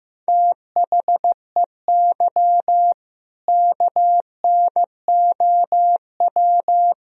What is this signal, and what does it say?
Morse code, heard as a keyed tone, about 15 wpm, 700 Hz. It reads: THEY KNOW
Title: Sentences from Top 500 Words and Abbreviations 15wpm